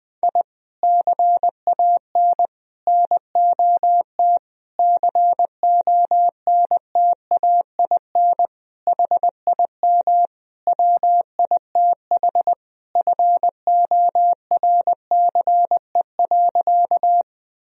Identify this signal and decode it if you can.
Morse code, heard as a keyed tone, about 20 wpm, 700 Hz. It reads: I CAN NOT CONTAIN HIM WITH FORCE.
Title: Sentences from Top 1000 Words - Review Part 3 - 20wpm